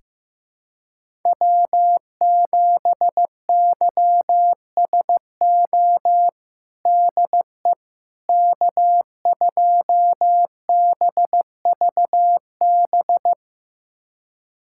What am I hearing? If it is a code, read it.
Morse code, heard as a keyed tone, about 15 wpm, 700 Hz. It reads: W7YSO DE K2BVB
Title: USA Callsign DE Callsign - 5 Characters - 15wpm